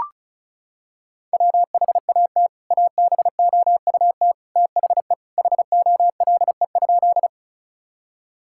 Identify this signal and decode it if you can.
Morse code, heard as a keyed tone, about 36 wpm, 700 Hz. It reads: WHAT ABOUT THE HOLE?
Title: Sentences from Top 1600 Words 35wpm